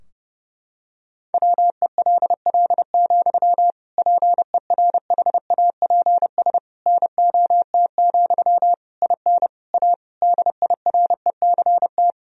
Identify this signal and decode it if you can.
Morse code, heard as a keyed tone, about 30 wpm, 700 Hz. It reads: WELL, PERHAPS NOT, IN A DIRECT
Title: Sentences from Top 1000 Words - Review Part 2 - 30wpm